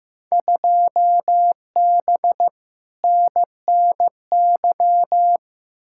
Morse code, heard as a keyed tone, about 15 wpm, 700 Hz. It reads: 2B NNY
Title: Field Day Exchanges Encore - 15wpm